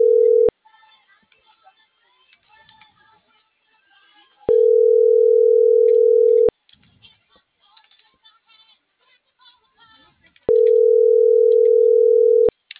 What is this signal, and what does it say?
Ringback tone (North America), 440 + 480 Hz.